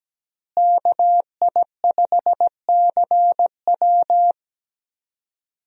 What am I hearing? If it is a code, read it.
Morse code, heard as a keyed tone, about 17 wpm, 700 Hz. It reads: KI5CW
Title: USA Call Signs - 5 Characters - Repeated 2x - 17wpm